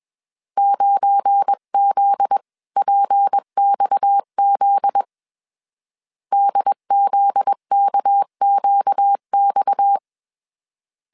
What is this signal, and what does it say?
Morse code, heard as a keyed tone, about 21 wpm, 800 Hz. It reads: 9ZPXZ DZKQX
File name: CW / DET / 1 9 1500C S C 20wpm 2310